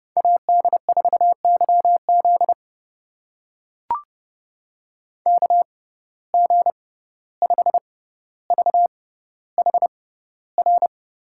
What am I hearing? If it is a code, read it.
Morse code, heard as a keyed tone, about 31 wpm, spaced as for 15 wpm, 700 Hz. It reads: AD4YZ KG5VHR
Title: USA Call Signs - 6x Character Spacing - 30wpm